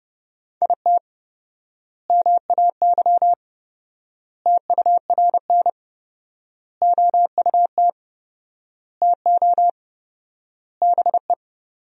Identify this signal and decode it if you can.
Morse code, heard as a keyed tone, about 31 wpm, 700 Hz. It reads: IT MAY TURN OUT TO BE
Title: 4x Word-Spacing - Sentences from Top 400 Words - 30wpm